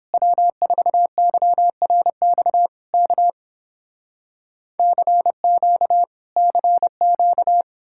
Morse code, heard as a keyed tone, about 31 wpm, 700 Hz. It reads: W4YRX K CQ CQ